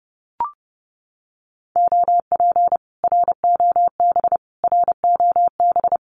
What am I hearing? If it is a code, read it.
Morse code, heard as a keyed tone, about 29 wpm, 700 Hz. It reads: OP ROB ROB
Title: Common QSO Phrases Encore 30wpm